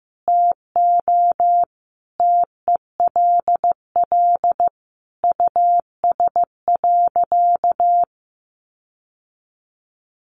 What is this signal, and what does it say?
Morse code, heard as a keyed tone, about 15 wpm, 700 Hz. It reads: TO TELL US.
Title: Sentences from Top 1400 Words 15wpm